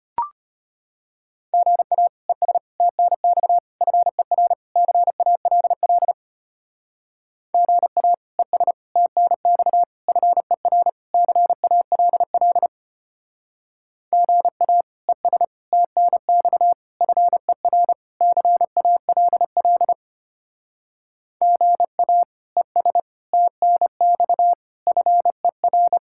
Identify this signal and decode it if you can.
Morse code, heard as a keyed tone, about 32 wpm, 700 Hz. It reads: GA ES TNX FER CALL GA ES TNX FER CALL GA ES TNX FER CALL GA ES TNX FER